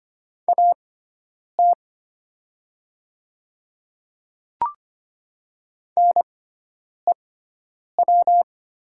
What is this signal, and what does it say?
Morse code, heard as a keyed tone, about 25 wpm, spaced as for 9 wpm, 700 Hz. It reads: AT NEW
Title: 6x Character Spacing - Top 200 Words - 25wpm